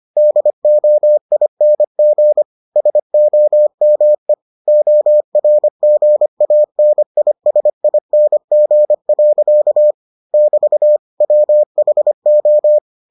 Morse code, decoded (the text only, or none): DOING SOME ORGANISING. <BT> WHO